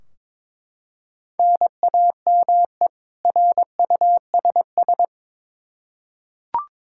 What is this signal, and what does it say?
Morse code, heard as a keyed tone, about 22 wpm, 700 Hz. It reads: NAME RUSS
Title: K1USN SST QSO Practice 22wpm